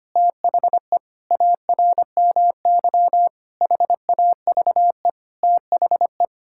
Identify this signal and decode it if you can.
Morse code, heard as a keyed tone, about 25 wpm, 700 Hz. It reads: THE ARMY HAVE THE
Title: Sentences from Top 1000 Words - Review Part 6 - 25wpm